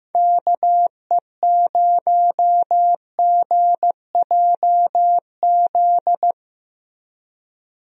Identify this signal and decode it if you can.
Morse code, heard as a keyed tone, about 15 wpm, 700 Hz. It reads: KE0GJZ
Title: USA Call Signs - 6 Characters - Repeated 2x - 15wpm